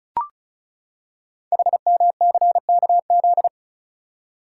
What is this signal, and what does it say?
Morse code, heard as a keyed tone, about 36 wpm, 690 Hz. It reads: HMCKZ
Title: Code Groups - 5 Letters - 35wpm